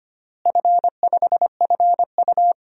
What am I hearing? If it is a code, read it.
Morse code, heard as a keyed tone, about 25 wpm, 700 Hz. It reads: F5FU
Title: DXCC-100 Call Signs - 4 Characters - Repeated 2x - 25wpm